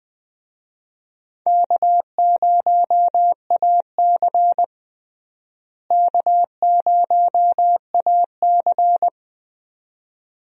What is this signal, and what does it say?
Morse code, heard as a keyed tone, about 20 wpm, 700 Hz. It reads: K0AC K0AC
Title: USA Call Signs - 4 Characters - Repeated 2x - 20wpm